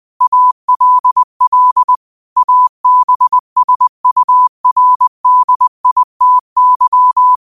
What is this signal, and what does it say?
Morse code, heard as a keyed tone, about 20 wpm, 1000 Hz. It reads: ALL ABSURDITY
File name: qotdfiles / 20wpm